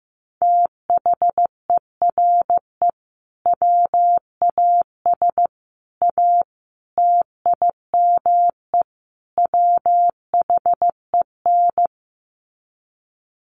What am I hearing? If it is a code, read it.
Morse code, heard as a keyed tone, about 15 wpm, 700 Hz. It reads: THERE WAS A TIME WHEN
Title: Sets of 5 Words 15wpm